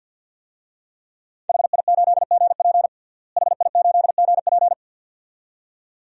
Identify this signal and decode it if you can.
Morse code, heard as a keyed tone, about 49 wpm, 700 Hz. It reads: HI8GP HI8GP
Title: DXCC-100 Call Signs - 5 Characters - Repeated 2x - 50wpm